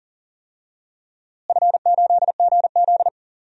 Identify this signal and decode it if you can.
Morse code, heard as a keyed tone, about 40 wpm, 700 Hz. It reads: F8GZ
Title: DXCC-100 Call Signs - 4 Characters - Repeated 2x - 40wpm